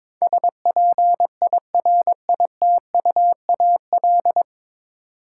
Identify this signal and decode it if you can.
Morse code, heard as a keyed tone, about 22 wpm, 700 Hz. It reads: SPIRITUAL
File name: practice / 22wpm / 9 Letter Words - First 3 Letters Given - 22wpm